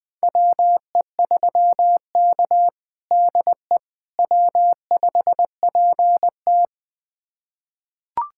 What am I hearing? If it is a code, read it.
Morse code, heard as a keyed tone, about 20 wpm, 700 Hz. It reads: WE3K DE W5PT
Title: USA Callsign DE Callsign - 4 Characters - 20wpm